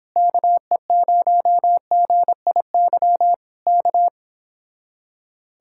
Morse code, heard as a keyed tone, about 26 wpm, 700 Hz. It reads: KE0GIY K
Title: Speed-Racing Common QSO Phrases - 22wpm